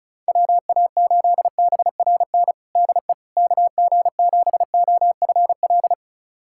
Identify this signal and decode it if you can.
Morse code, heard as a keyed tone, about 34 wpm, 700 Hz. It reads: WA8BRN DE KG7OFL
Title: USA Callsign DE Callsign - 6 Characters - 35wpm